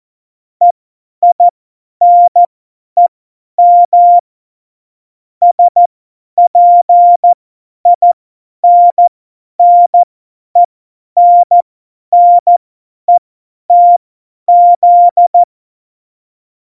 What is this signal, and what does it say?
Morse code, heard as a keyed tone, about 14 wpm, spaced as for 10 wpm, 700 Hz. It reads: EINEM SPINNENNETZ
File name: Morsekurs / Witze / Witz 24 April 2018, 50 F70